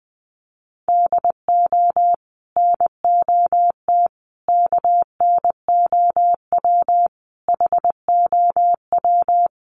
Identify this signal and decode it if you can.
Morse code, heard as a keyed tone, about 19 wpm, 700 Hz. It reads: DO NOT KNOW HOW
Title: Sets of 5 Words Encore2 20wpm